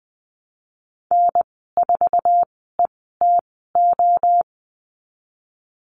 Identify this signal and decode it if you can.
Morse code, heard as a keyed tone, about 20 wpm, spaced as for 14 wpm, 700 Hz. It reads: N4ETO
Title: 088 Lesson 15g - Number 5 - Call Signs 20wpm